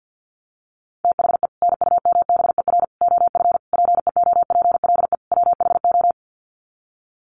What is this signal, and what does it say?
Morse code, heard as a keyed tone, about 48 wpm, 700 Hz. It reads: THE NUMBER OF PEOPLE WHO
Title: Sets of 5 Words Encore 50wpm